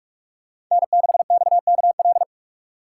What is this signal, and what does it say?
Morse code, heard as a keyed tone, about 40 wpm, 700 Hz. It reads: N6XKL